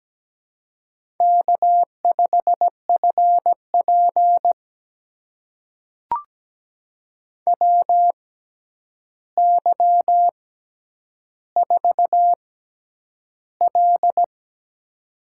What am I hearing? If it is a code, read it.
Morse code, heard as a keyed tone, about 17 wpm, spaced as for 9 wpm, 700 Hz. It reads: K5FP WY4L
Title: USA Call Signs - 4 Characters - 6x Character Spacing - 17wpm